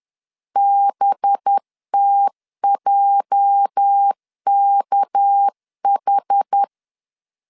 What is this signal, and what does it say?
Morse code, heard as a keyed tone, about 10 wpm, 800 Hz. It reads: BTJKH
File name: CW / DET / 1 9 1500C S B 10wpm 4439